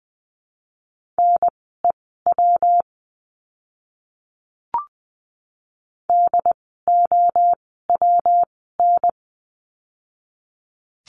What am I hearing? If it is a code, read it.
Morse code, heard as a keyed tone, about 20 wpm, spaced as for 15 wpm, 700 Hz. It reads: NEW DOWN
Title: 160 Lesson 26f - Letter g - Words 20wpm